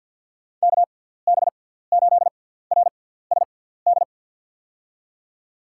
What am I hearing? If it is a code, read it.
Morse code, heard as a keyed tone, about 52 wpm, spaced as for 17 wpm, 700 Hz. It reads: KB8RSD
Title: USA Call Signs - 6 Characters - 6x Character Spacing - 50wpm